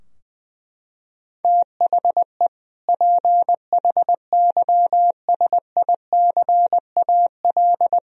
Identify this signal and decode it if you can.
Morse code, heard as a keyed tone, about 20 wpm, 700 Hz. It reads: THE PHYSICAL
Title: Sentences from Top 1000 Words - Review Part 4 - 20wpm